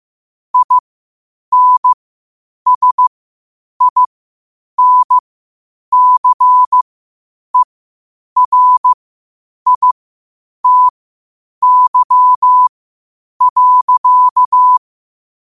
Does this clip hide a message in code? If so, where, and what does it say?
Yes: Morse code from 0.5 to 14.8 s, heard as a keyed tone, about 15 wpm, spaced as for 8 wpm, 1000 Hz. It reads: INSINCERITY.